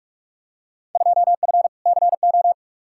Morse code, heard as a keyed tone, about 43 wpm, 700 Hz. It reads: 2F CO
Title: Field Day Exchanges - 45wpm